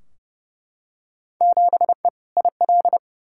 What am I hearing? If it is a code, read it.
Morse code, heard as a keyed tone, about 32 wpm, 700 Hz. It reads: 7E IL
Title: Field Day Exchanges Encore - 30wpm